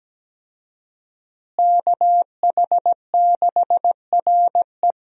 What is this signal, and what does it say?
Morse code, heard as a keyed tone, about 17 wpm, 700 Hz. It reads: KH6RE
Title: USA Call Signs - 5 Characters - Repeated 2x - 17wpm